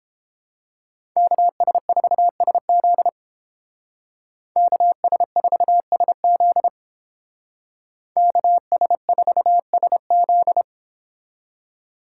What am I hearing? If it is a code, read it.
Morse code, heard as a keyed tone, about 30 wpm, 700 Hz. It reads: KS4SZ KS4SZ KS4SZ